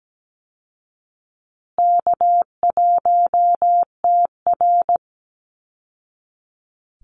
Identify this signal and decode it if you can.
Morse code, heard as a keyed tone, about 17 wpm, 700 Hz. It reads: K1TR
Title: Easy Call Signs Repeated 3x - 17wpm